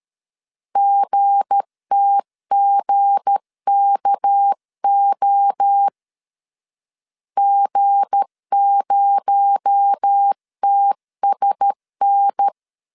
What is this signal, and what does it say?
Morse code, heard as a keyed tone, about 13 wpm, 800 Hz. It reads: GTGKO G0TSN
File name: LEZIONI / 7 GF9Z / 7 G 12wpm 2055